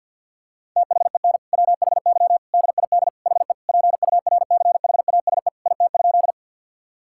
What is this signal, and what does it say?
Morse code, heard as a keyed tone, about 50 wpm, 700 Hz. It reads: THEN WHY DID HE PURCHASE IT?
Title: Sentences from Top 1000 Words - Review Part 5 - 50wpm